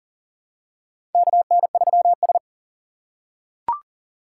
Morse code, heard as a keyed tone, about 38 wpm, 700 Hz. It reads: KN3S
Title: USA Call Signs - 4 Characters - 3x Character Spacing - 40wpm